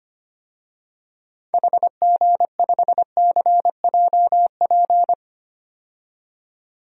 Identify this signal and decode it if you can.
Morse code, heard as a keyed tone, about 25 wpm, 700 Hz. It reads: HG5CJP
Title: DXCC-100 Call Signs - 6 Characters - 25wpm